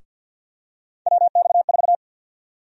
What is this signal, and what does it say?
Morse code, heard as a keyed tone, about 52 wpm, 700 Hz. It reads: WX4
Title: USA Call Sign Prefixes - 50wpm